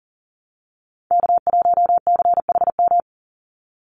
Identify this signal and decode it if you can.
Morse code, heard as a keyed tone, about 40 wpm, 700 Hz. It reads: K1CHM
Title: Call Signs Repeated 2x - 40wpm